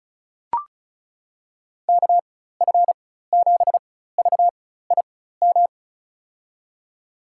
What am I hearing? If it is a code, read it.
Morse code, heard as a keyed tone, about 34 wpm, spaced as for 16 wpm, 700 Hz. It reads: KF7VIM
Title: USA Call Signs - 6 Characters - 4x Character Spacing - 35wpm